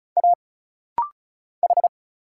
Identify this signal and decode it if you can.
Morse code, heard as a keyed tone, about 34 wpm, 700 Hz. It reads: A H
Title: Single Letters - Rapid-Fire 35wpm